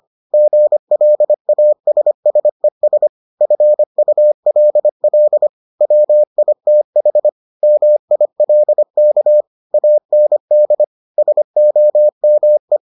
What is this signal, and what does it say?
Morse code, heard as a keyed tone, about 25 wpm, 600 Hz. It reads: GLASSES FULL WITH MILK AND SOME